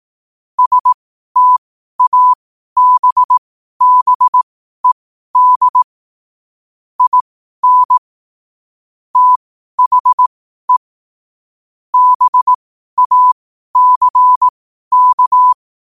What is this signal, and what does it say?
Morse code, heard as a keyed tone, about 18 wpm, spaced as for 12 wpm, 1000 Hz. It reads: STABBED IN THE BACK
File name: qotdfiles / 12wpm